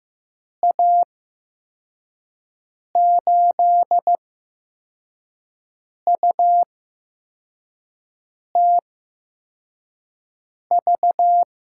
Morse code, heard as a keyed tone, about 15 wpm, spaced as for 4 wpm, 700 Hz. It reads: A8UTV
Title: USA Call Signs - 6 Characters - 8x Character Spacing - 15wpm